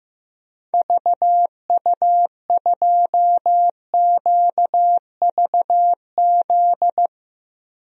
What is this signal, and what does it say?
Morse code, heard as a keyed tone, about 15 wpm, 700 Hz. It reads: VU2QVZ
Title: DXCC-100 Call Signs - 6 Characters - Repeated 3x - 15wpm